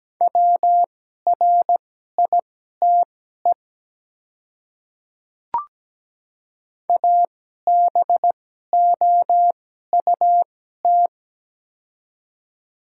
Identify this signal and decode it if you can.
Morse code, heard as a keyed tone, about 17 wpm, spaced as for 12 wpm, 700 Hz. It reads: WRITE ABOUT
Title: 2x Character Spacing - Top 300 Words - 17wpm